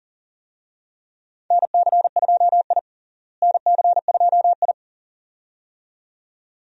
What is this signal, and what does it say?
Morse code, heard as a keyed tone, about 40 wpm, 700 Hz. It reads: NC2I NC2I
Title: USA Call Signs - 4 Characters - Repeated 2x - 40wpm